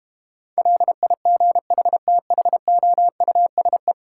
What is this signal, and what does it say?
Morse code, heard as a keyed tone, about 32 wpm, 700 Hz. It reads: LIGHTHOUSE